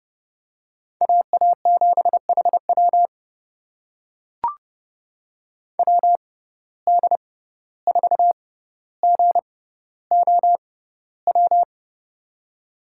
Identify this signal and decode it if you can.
Morse code, heard as a keyed tone, about 30 wpm, spaced as for 15 wpm, 700 Hz. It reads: AA7HW WD4GOW
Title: USA Call Signs - 6x Character Spacing - 30wpm